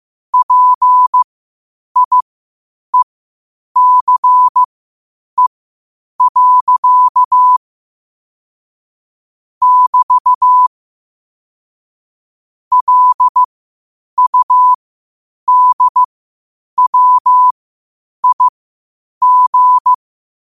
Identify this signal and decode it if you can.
Morse code, heard as a keyed tone, about 15 wpm, spaced as for 8 wpm, 1000 Hz. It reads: PIECE. <BT> LUDWIG
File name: qotdfiles / 8wpm